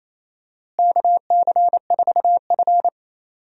Morse code, heard as a keyed tone, about 27 wpm, 700 Hz. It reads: KC4F
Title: USA Call Signs - 4 Characters - Repeated 2x - 28wpm